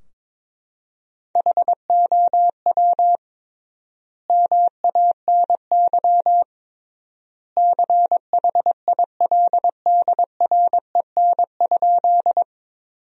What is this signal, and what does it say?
Morse code, heard as a keyed tone, about 22 wpm, 700 Hz. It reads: HOW MANY CHILDREN?
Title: 3x Word-Spacing - Sentences from Top 300 Words - 22wpm